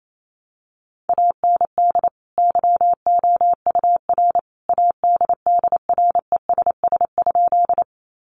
Morse code, heard as a keyed tone, about 28 wpm, 700 Hz. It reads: AND YOUR ADDRESS?
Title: Sentences from Top 1000 Words 28wpm